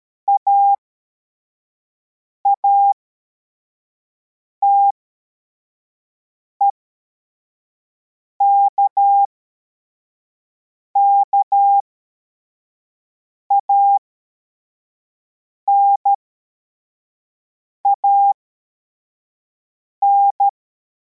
Morse code, heard as a keyed tone, about 13 wpm, spaced as for 4 wpm, 800 Hz. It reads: AATEK KANAN